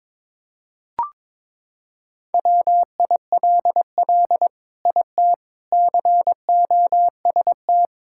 Morse code, heard as a keyed tone, about 22 wpm, 700 Hz. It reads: WILL IT COST
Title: Sentences from Top 1000 Words - Review Part 5 - 22wpm